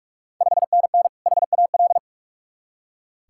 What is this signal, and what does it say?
Morse code, heard as a keyed tone, about 46 wpm, 700 Hz. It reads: 5NN HAL
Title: Ontario QSO Party Exchange 45wpm